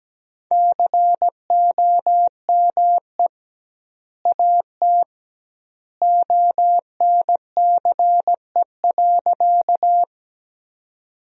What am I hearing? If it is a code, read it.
Morse code, heard as a keyed tone, about 17 wpm, 700 Hz. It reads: COME AT ONCE.
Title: 2x Word-Spacing - Sentences from Top 300 Words - 17wpm